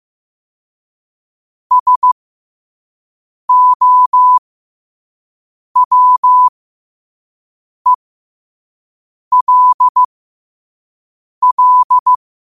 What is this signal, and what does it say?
Morse code, heard as a keyed tone, about 15 wpm, spaced as for 5 wpm, 1000 Hz. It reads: SOWELL
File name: qotdfiles / 5wpm